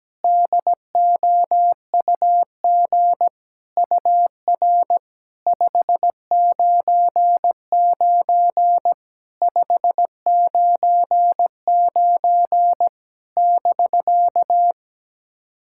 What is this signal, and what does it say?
Morse code, heard as a keyed tone, about 17 wpm, 700 Hz. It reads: DOUG UR 599 599 <BK>
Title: SOTA QSO Practice 17wpm